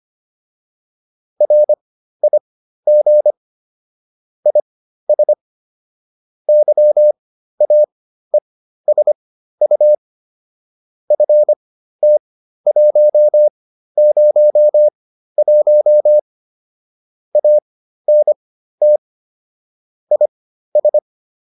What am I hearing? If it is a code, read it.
Morse code, heard as a keyed tone, about 25 wpm, spaced as for 13 wpm, 600 Hz. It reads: RIG IS YAESU FT101 ANT IS